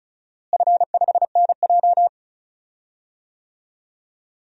Morse code, heard as a keyed tone, about 36 wpm, 700 Hz. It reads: F5NJ
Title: DXCC-100 Call Signs - 4 Characters - Repeated 3x - 35wpm